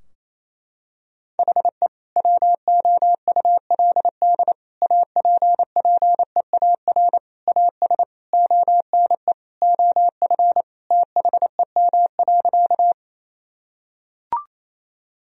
Morse code, heard as a keyed tone, about 28 wpm, 700 Hz. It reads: HE WOULD APPEAR AS ONE OF THEM.
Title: Sentences from Top 1000 Words - Review Part 2 - 28wpm